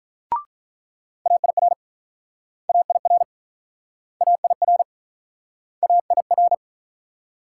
Morse code, heard as a keyed tone, about 48 wpm, 700 Hz. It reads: AIR AIR AIR AIR